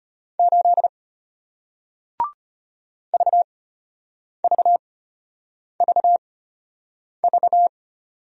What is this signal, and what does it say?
Morse code, heard as a keyed tone, about 34 wpm, spaced as for 8 wpm, 700 Hz. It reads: 8 VVVV